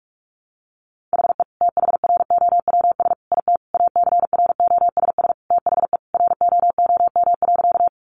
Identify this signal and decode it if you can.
Morse code, heard as a keyed tone, about 44 wpm, 700 Hz. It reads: HE THROWS IT ACROSS THE ROOM.
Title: Additional Easy Phrases 45wpm